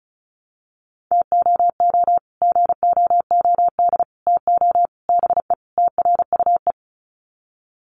Morse code, heard as a keyed tone, about 34 wpm, 700 Hz. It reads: TOO GOOD TO BE TRUE
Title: Sets of 5 Words Encore 35wpm